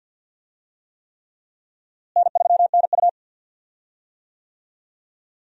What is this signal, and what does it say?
Morse code, heard as a keyed tone, about 51 wpm, 700 Hz. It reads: N3NU
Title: USA Call Signs - 4 Characters - Repeated 3x - 50wpm